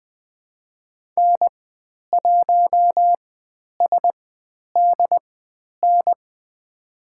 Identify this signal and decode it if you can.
Morse code, heard as a keyed tone, about 20 wpm, spaced as for 10 wpm, 700 Hz. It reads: N1SDN
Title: 102 Lesson 17g - Letter u - Call Signs 20wpm